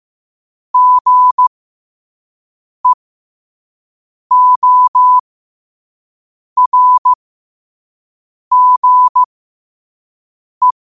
Morse code, heard as a keyed tone, about 15 wpm, spaced as for 5 wpm, 1000 Hz. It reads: GEORGE
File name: qotdfiles / 5wpm